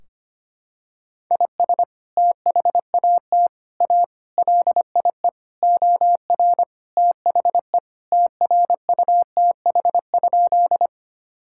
Morse code, heard as a keyed tone, about 25 wpm, 700 Hz. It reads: IS THAT A LIE OR THE TRUTH?
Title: Sentences from Top 1600 Words 25wpm